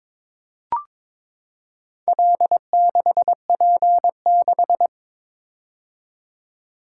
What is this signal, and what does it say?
Morse code, heard as a keyed tone, about 22 wpm, 700 Hz. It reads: L6P6
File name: practice / 22wpm / Four Letter-Number Combinations - 22wpm